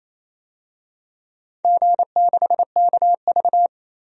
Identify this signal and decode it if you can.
Morse code, heard as a keyed tone, about 28 wpm, 700 Hz. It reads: G6KV
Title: DXCC-100 Call Signs - 4 Characters - 28wpm